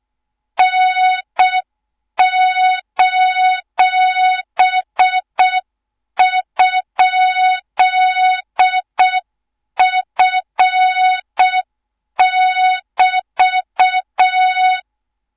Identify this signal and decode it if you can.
Morse code, heard as a keyed tone, about 6 wpm, 3700 Hz. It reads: N:?F<BT>
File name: Morze-HA2MN / hang / 24 GYAKORLAT